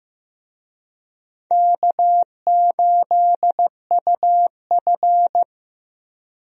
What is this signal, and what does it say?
Morse code, heard as a keyed tone, about 15 wpm, 700 Hz. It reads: K8UF
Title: USA Call Signs - 4 Characters - Repeated 2x - 15wpm